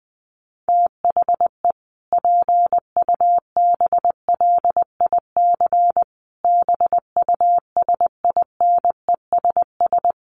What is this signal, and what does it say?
Morse code, heard as a keyed tone, about 20 wpm, 700 Hz. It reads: THE PUBLIC BUSINESS
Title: Sentences from Top 1300 Words 20wpm